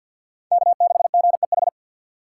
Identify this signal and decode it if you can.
Morse code, heard as a keyed tone, about 48 wpm, 700 Hz. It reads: K6GEH
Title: USA Call Signs - 5 Characters - 2x Character Spacing - 50wpm